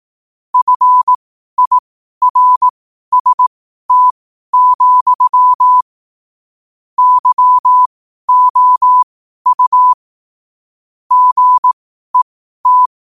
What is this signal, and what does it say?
Morse code, heard as a keyed tone, about 18 wpm, spaced as for 12 wpm, 1000 Hz. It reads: FIRST, YOU GET
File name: qotdfiles / 12wpm